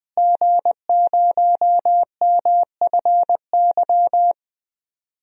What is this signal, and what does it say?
Morse code, heard as a keyed tone, about 21 wpm, 700 Hz. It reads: G0MFY
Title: DXCC-100 Call Signs - 5 Characters - Repeated 3x - 20wpm